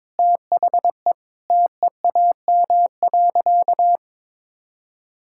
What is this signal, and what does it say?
Morse code, heard as a keyed tone, about 22 wpm, 700 Hz. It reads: THE TEAM.
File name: practice / 22wpm / Sentences with 3-Word Phrases - 22wpm